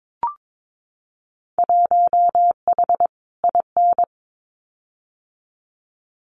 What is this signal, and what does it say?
Morse code, heard as a keyed tone, about 22 wpm, 700 Hz. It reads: 1H IN
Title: Winter Field Day Exchanges 22wpm